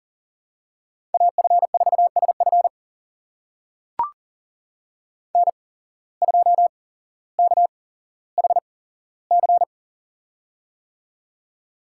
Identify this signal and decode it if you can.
Morse code, heard as a keyed tone, about 40 wpm, spaced as for 17 wpm, 700 Hz. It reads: AF4SF N2KHC
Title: USA Call Signs - 5 Characters - 8x Character Spacing - 40wpm